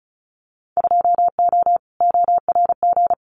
Morse code, heard as a keyed tone, about 35 wpm, 690 Hz. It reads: 2O ORG
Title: Winter Field Day Exchanges 35wpm